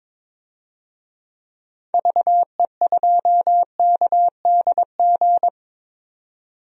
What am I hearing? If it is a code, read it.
Morse code, heard as a keyed tone, about 22 wpm, 700 Hz. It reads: VE2KDG
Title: DXCC-100 Call Signs - 6 Characters - 22wpm